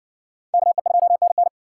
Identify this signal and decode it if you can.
Morse code, heard as a keyed tone, about 56 wpm, 700 Hz. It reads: KE2NN